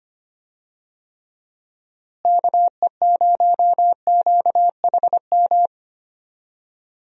Morse code, heard as a keyed tone, about 25 wpm, 700 Hz. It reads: KE0QHM
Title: USA Call Signs - 6 Characters - Repeated 2x - 25wpm